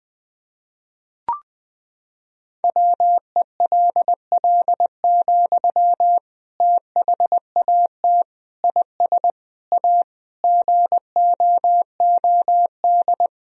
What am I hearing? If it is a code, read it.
Morse code, heard as a keyed tone, about 20 wpm, 700 Hz. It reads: WELL, THAT IS A GOOD
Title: Sentences from Top 1000 Words - Review Part 2 - 20wpm